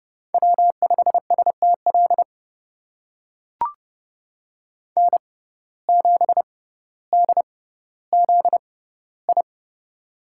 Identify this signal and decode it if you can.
Morse code, heard as a keyed tone, about 30 wpm, spaced as for 15 wpm, 700 Hz. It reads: W5STL N7DZI
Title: USA Call Signs - 5 Characters - 6x Character Spacing - 30wpm